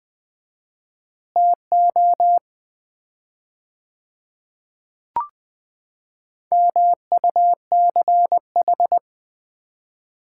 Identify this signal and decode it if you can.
Morse code, heard as a keyed tone, about 20 wpm, 700 Hz. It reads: TO MUCH
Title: 4x Word-Spacing - Sets of 5 Words - 20wpm